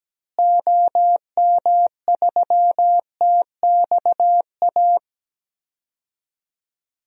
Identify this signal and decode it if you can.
Morse code, heard as a keyed tone, about 17 wpm, 700 Hz. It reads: OM3TXA
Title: DXCC-100 Call Signs - 6 Characters - Repeated 2x - 17wpm